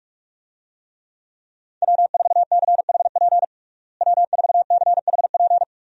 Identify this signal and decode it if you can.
Morse code, heard as a keyed tone, about 41 wpm, 700 Hz. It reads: W4CHP W4CHP
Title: USA Call Signs - 5 Characters - Repeated 2x - 45wpm